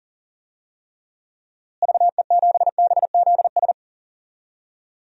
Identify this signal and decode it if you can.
Morse code, heard as a keyed tone, about 39 wpm, 700 Hz. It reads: VE7BZS
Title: DXCC-100 Call Signs - 6 Characters - Repeated 2x - 40wpm